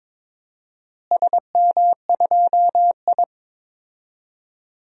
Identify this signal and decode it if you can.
Morse code, heard as a keyed tone, about 22 wpm, 700 Hz. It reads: SM2I
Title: DXCC-100 Call Signs - 4 Characters - Repeated 3x - 22wpm